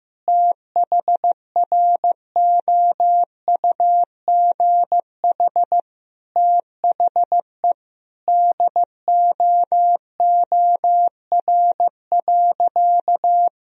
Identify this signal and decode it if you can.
Morse code, heard as a keyed tone, about 16 wpm, 700 Hz. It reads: THROUGH THE DOOR.
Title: Sentences from Top 400 Words and Abbreviations 15wpm